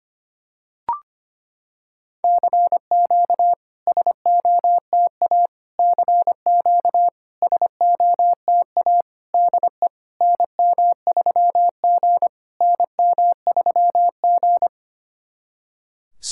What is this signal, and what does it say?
Morse code, heard as a keyed tone, about 25 wpm, 690 Hz. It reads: CQ SOTA CQ SOTA DE NM3G NM3G
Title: SOTA QSO Practice 25wpm